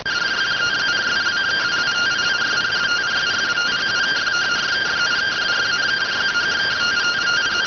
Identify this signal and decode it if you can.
Two-tone teleprinter signal, about 170 Hz shift.